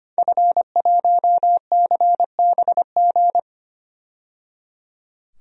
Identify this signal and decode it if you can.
Morse code, heard as a keyed tone, about 25 wpm, 700 Hz. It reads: F1CBG
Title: DXCC-100 Call Signs - 5 Characters - 25wpm